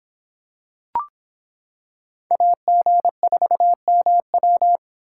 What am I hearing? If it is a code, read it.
Morse code, heard as a keyed tone, about 25 wpm, 700 Hz. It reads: AG4MW